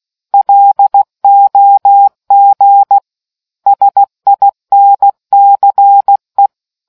Morse code, heard as a keyed tone, about 16 wpm, 800 Hz. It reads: LOG SINCE